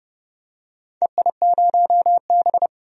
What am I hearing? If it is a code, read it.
Morse code, heard as a keyed tone, about 31 wpm, 700 Hz. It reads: EI0B
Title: DXCC-100 Call Signs - 4 Characters - 30wpm